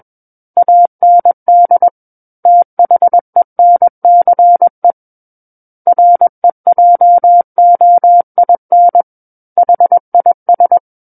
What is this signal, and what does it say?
Morse code, heard as a keyed tone, about 21 wpm, 700 Hz. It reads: AND THENCE REJOIN HIS
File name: Book One / 12-Chapter 12 - 21 WPM